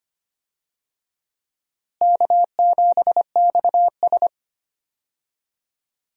Morse code, heard as a keyed tone, about 25 wpm, 700 Hz. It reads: K7XS
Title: USA Call Signs - 4 Characters - Repeated 2x - 25wpm